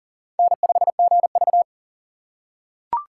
Morse code, heard as a keyed tone, about 40 wpm, 700 Hz. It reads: N5GV
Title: USA Call Signs - 4 Characters - 2x Character Spacing - 40wpm